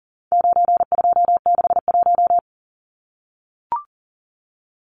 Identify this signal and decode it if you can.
Morse code, heard as a keyed tone, about 40 wpm, 700 Hz. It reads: 9261
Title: Numbers - 4-Digit 40wpm